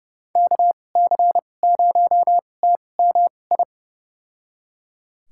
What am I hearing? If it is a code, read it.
Morse code, heard as a keyed tone, about 31 wpm, spaced as for 22 wpm, 700 Hz. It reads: KC0TMI
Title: USA Call Signs - 6 Characters - 2x Character Spacing - 30wpm